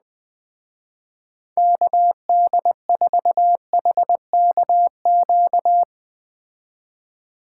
Morse code, heard as a keyed tone, about 19 wpm, 700 Hz. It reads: KD4HKQ
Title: Speed Racing - USA 2x3 Call Signs 17wpm